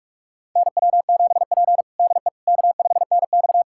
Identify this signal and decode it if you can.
Morse code, heard as a keyed tone, about 46 wpm, 700 Hz. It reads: NW7P DE K5NX
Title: USA Callsign DE Callsign - 4 Characters - 45wpm